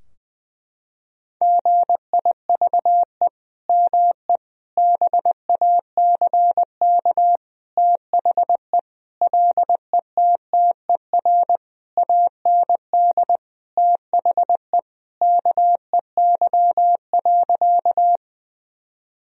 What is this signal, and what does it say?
Morse code, heard as a keyed tone, about 20 wpm, 700 Hz. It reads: GIVE ME BACK THE LETTER AND THE KEY.
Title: Sentences from Top 1000 Words - Review Part 4 - 20wpm